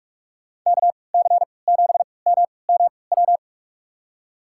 Morse code, heard as a keyed tone, about 44 wpm, spaced as for 25 wpm, 700 Hz. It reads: KC7MMW
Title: USA Call Signs - 6 Characters - 3x Character Spacing - 45wpm